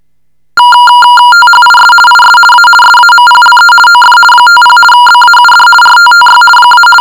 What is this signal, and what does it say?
RTTY, about 50 baud. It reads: THIS IS A SAMPLE OF 50 BAUD RTTY -
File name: shortwave audio / RTTY 50 baud sample